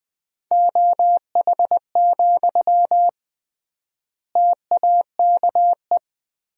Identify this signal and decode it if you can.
Morse code, heard as a keyed tone, about 20 wpm, 700 Hz. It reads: OH, TAKE
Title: 3x Word-Spacing - Sentences from Top 500 Words - 20wpm